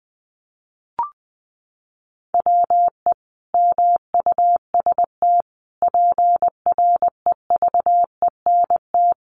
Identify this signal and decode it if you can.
Morse code, heard as a keyed tone, about 20 wpm, 700 Hz. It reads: WE MUST PREVENT
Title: Sentences from Top 1100 Words 20wpm